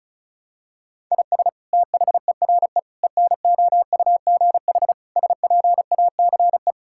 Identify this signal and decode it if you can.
Morse code, heard as a keyed tone, about 36 wpm, 700 Hz. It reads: IS THERE ENOUGH SPACE
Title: Sentences from Top 1000 Words - Review Part 5 - 35wpm